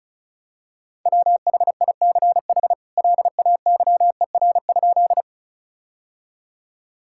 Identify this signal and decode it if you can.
Morse code, heard as a keyed tone, about 36 wpm, 700 Hz. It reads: WHICH LAYER?
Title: Sentences from Top 1600 Words 35wpm